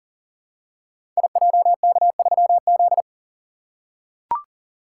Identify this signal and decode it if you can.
Morse code, heard as a keyed tone, about 40 wpm, 690 Hz. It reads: IJK3Z
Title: Code Groups - 5 Letter:Numbers - 40wpm